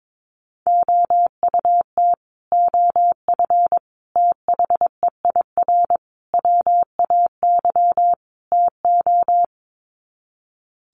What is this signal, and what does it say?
Morse code, heard as a keyed tone, about 22 wpm, 700 Hz. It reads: OUT OF THEIR WAY TO
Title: Sets of 5 Words Encore2 22wpm